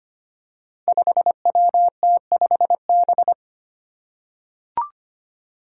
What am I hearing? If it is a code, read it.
Morse code, heard as a keyed tone, about 25 wpm, 700 Hz. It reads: 5WT5B
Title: Five Challenging Letter-Number Combinations 25wpm